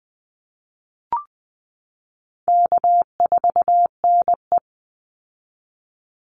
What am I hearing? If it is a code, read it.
Morse code, heard as a keyed tone, about 20 wpm, 700 Hz. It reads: K4NE
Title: Easy Call Signs Repeated 2x - 20wpm